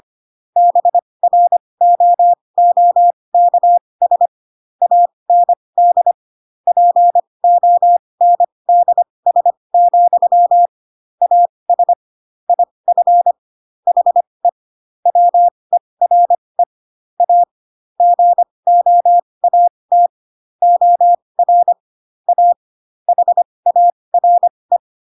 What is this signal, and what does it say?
Morse code, heard as a keyed tone, about 25 wpm, spaced as for 20 wpm, 700 Hz. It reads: BROOKS AND PONDS, AS IF HE WERE A GOAT OR A HARE
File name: Pinocchio / Pinocchio ch4 15 25 ik1hge